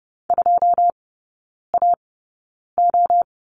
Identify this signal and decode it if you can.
Morse code, heard as a keyed tone, about 30 wpm, spaced as for 9 wpm, 700 Hz. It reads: 2AO
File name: Lesson 16e - Letter c - Three Characters 30wpm